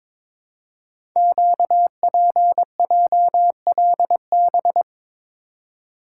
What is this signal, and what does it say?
Morse code, heard as a keyed tone, about 22 wpm, 700 Hz. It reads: QPJLB
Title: Code Groups - 5 Letters - 22wpm